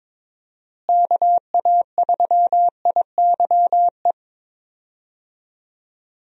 Morse code, heard as a keyed tone, about 22 wpm, 700 Hz. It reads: KA3IYE
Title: USA Call Signs - 6 Characters - Repeated 2x - 22wpm